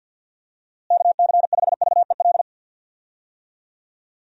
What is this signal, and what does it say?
Morse code, heard as a keyed tone, about 48 wpm, 700 Hz. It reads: KC5VEL
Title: USA Call Signs - 6 Characters - Repeated 3x - 50wpm